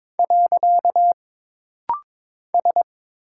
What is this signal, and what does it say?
Morse code, heard as a keyed tone, about 22 wpm, 700 Hz. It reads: . S
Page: This rapid-fire timing is geared to push your ICR (Instant Character Recognition) skills.